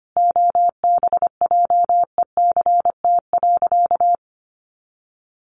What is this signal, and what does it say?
Morse code, heard as a keyed tone, about 25 wpm, 700 Hz. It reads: OBJECT.
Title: Sentences from Top 800 Words 25wpm